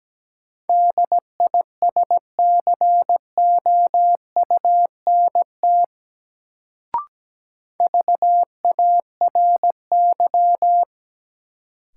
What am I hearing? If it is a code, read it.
Morse code, heard as a keyed tone, about 17 wpm, 700 Hz. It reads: DISCOUNT VARY
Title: Sentences from Top 1200 Words - New Word Drill - 17wpm